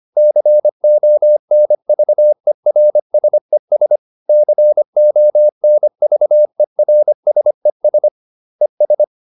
Morse code, heard as a keyed tone, about 25 wpm, 600 Hz. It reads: CONVERSES CONVERSES ES